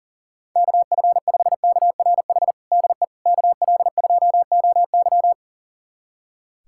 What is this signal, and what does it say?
Morse code, heard as a keyed tone, about 40 wpm, 700 Hz. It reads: KF5KRH DE KL2OY
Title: USA Callsign DE Callsign 40wpm